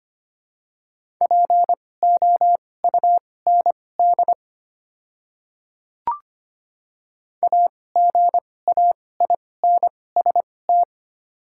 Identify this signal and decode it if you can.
Morse code, heard as a keyed tone, about 25 wpm, spaced as for 18 wpm, 700 Hz. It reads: POUND AGAINST
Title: 2x Character Spacing - Top 500 Words - 25wpm